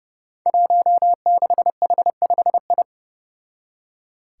Morse code, heard as a keyed tone, about 29 wpm, 700 Hz. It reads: 16H5I
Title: Five Challenging Letter-Number Combinations 30wpm